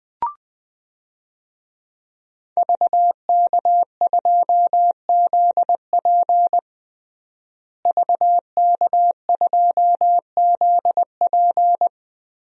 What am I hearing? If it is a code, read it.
Morse code, heard as a keyed tone, about 21 wpm, 700 Hz. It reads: VK2ZP VK2ZP
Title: DXCC-100 Call Signs - 5 Characters - Repeated 2x - 20wpm